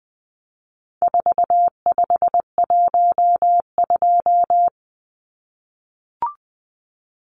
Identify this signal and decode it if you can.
Morse code, heard as a keyed tone, about 20 wpm, 700 Hz. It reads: 4512
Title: Numbers - 4-Digit 20wpm